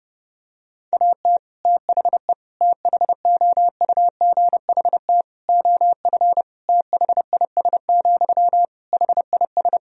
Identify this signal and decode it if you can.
Morse code, heard as a keyed tone, about 30 wpm, 700 Hz. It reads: AT THE THOUGHT OF THIS, HIS
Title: Sentences from Top 1000 Words - Review Part 2 - 30wpm